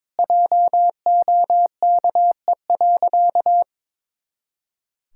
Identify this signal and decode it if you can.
Morse code, heard as a keyed tone, about 22 wpm, 700 Hz. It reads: JOKE.
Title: Pangram Challenge v2 - 22wpm